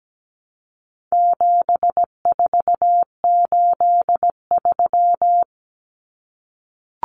Morse code, heard as a keyed tone, about 17 wpm, 700 Hz. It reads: 7483
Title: Common Cut Numbers - 4-Digit 17wpm